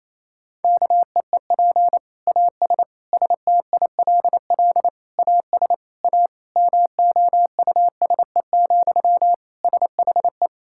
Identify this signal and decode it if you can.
Morse code, heard as a keyed tone, about 28 wpm, 700 Hz. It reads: KEEP AS STILL AS A MOUSE, SHE
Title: Sentences from Top 1000 Words - Review Part 5 - 28wpm